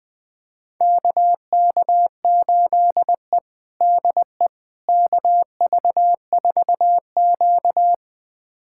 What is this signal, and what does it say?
Morse code, heard as a keyed tone, about 20 wpm, 700 Hz. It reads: KK8E DE KV4Q
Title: USA Callsign DE Callsign - 4 Characters - 20wpm